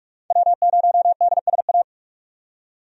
Morse code, heard as a keyed tone, about 47 wpm, 700 Hz. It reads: W0DSA
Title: USA Call Signs - 5 Characters - 45wpm